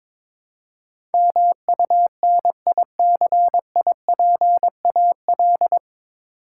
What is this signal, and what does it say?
Morse code, heard as a keyed tone, about 22 wpm, 700 Hz. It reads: MUNICIPAL
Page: Morse Code Ninja practice 22wpm